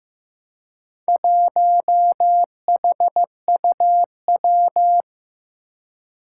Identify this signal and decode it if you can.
Morse code, heard as a keyed tone, about 15 wpm, 700 Hz. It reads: 1HUW
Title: Four Challenging Letter-Number Combinations 15wpm